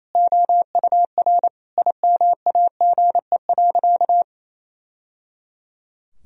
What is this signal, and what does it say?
Morse code, heard as a keyed tone, about 28 wpm, 700 Hz. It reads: OUR IMAGE.
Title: Sentences from Top 1000 Words - Review Part 4 - 28wpm